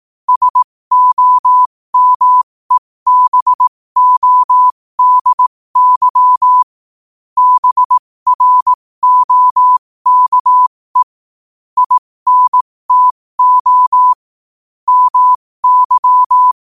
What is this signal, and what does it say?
Morse code, heard as a keyed tone, about 18 wpm, spaced as for 15 wpm, 1000 Hz. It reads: SOMEBODY BROKE INTO MY
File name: qotdfiles / 15wpm